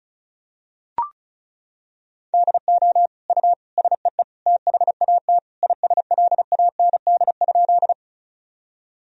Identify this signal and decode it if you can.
Morse code, heard as a keyed tone, about 34 wpm, 700 Hz. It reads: DO U SEE THAT ISLAND?
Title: Sentences from Top 500 Words and Abbreviations 35wpm